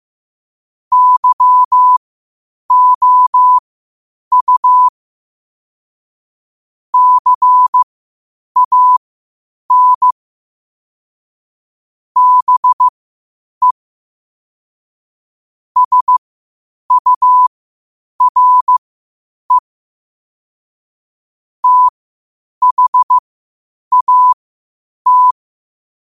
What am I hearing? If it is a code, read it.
Morse code, heard as a keyed tone, about 15 wpm, spaced as for 8 wpm, 1000 Hz. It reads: YOU CAN BE SURE THAT
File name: qotdfiles / 8wpm